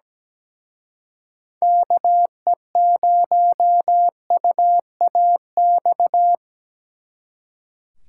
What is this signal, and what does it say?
Morse code, heard as a keyed tone, about 20 wpm, 700 Hz. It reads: KE0UAX
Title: Speed Racing - USA 2x3 Call Signs 17wpm